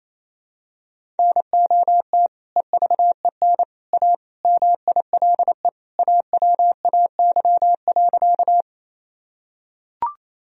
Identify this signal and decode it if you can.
Morse code, heard as a keyed tone, about 28 wpm, 700 Hz. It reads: NOT EVEN A MILE AWAY.
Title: Sentences from Top 1000 Words - Review Part 1 - 28wpm